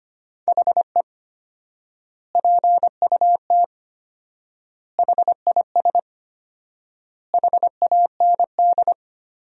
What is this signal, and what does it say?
Morse code, heard as a keyed tone, about 25 wpm, 700 Hz. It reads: HE PUT HIS HAND
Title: 4x Word-Spacing - Sentences from Top 400 Words - 25wpm